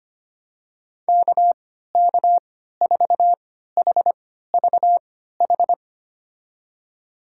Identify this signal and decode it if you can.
Morse code, heard as a keyed tone, about 25 wpm, spaced as for 14 wpm, 700 Hz. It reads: KK4HVH
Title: USA Call Signs - 6 Characters - 3x Character Spacing - 25wpm